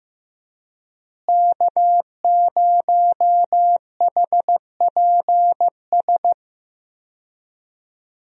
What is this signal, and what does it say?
Morse code, heard as a keyed tone, about 15 wpm, 700 Hz. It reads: K0HPS
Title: USA Call Signs - 5 Characters - Repeated 3x - 15wpm